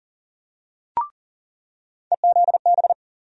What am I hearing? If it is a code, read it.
Morse code, heard as a keyed tone, about 40 wpm, 700 Hz. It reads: EZB
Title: USA Call Sign Postfixes - 3 Letters 40wpm